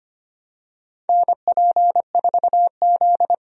Morse code, heard as a keyed tone, about 25 wpm, 700 Hz. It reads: NP4Z
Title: USA Call Signs - 4 Characters - 25wpm